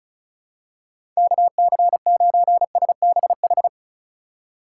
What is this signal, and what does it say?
Morse code, heard as a keyed tone, about 36 wpm, 700 Hz. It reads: KC9SBH
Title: USA Call Signs - 6 Characters - 2x Character Spacing - 35wpm